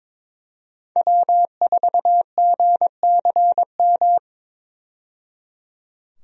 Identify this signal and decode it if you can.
Morse code, heard as a keyed tone, about 22 wpm, 700 Hz. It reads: W4GCM
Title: USA Call Signs - 5 Characters - 22wpm